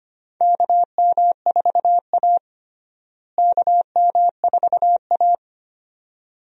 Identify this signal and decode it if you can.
Morse code, heard as a keyed tone, about 25 wpm, 700 Hz. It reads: KM4A KM4A
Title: USA Call Signs - 4 Characters - Repeated 2x - 25wpm